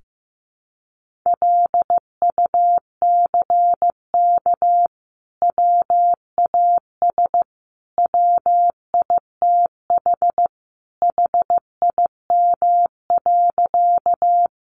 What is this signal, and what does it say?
Morse code, heard as a keyed tone, about 15 wpm, 700 Hz. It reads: LUCK WAS WITH HIM.
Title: Sentences from Top 1500 Words 15wpm